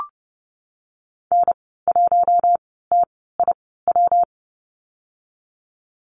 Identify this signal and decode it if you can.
Morse code, heard as a keyed tone, about 30 wpm, spaced as for 17 wpm, 700 Hz. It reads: N1TIW
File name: Lesson 19g - Letter w - Call Signs 30wpm